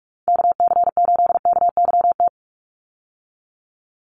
Morse code, heard as a keyed tone, about 44 wpm, 700 Hz. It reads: KC8KYT
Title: USA Call Signs 45wpm